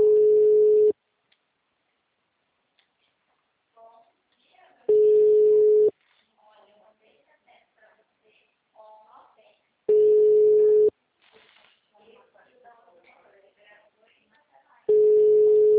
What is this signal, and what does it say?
Ringback tone (Europe, 425 Hz), with speech.